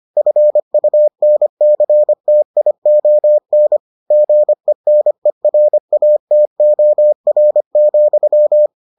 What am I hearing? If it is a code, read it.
Morse code, heard as a keyed tone, about 25 wpm, 600 Hz. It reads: FUNCTION GENERATOR,